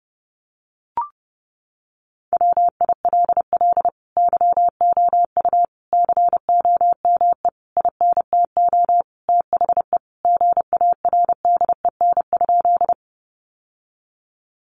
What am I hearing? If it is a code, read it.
Morse code, heard as a keyed tone, about 30 wpm, 700 Hz. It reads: WILL YOU COME INTO THE GARDEN?
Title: Sentences from Top 700 Words 30wpm